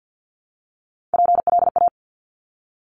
Morse code, heard as a keyed tone, about 50 wpm, 700 Hz. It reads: PLA
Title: Top Three Letter Combinations in Words 50wpm